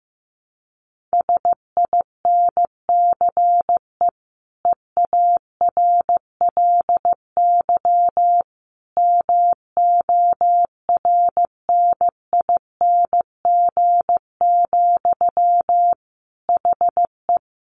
Morse code, heard as a keyed tone, about 15 wpm, 700 Hz. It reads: SINCE EARLY MORNING, HE
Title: Sentences from Top 1400 Words 15wpm